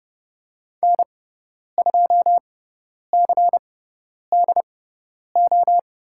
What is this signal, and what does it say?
Morse code, heard as a keyed tone, about 30 wpm, spaced as for 10 wpm, 700 Hz. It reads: N2CDO
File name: Lesson 16g - Letter c - Call Signs 30wpm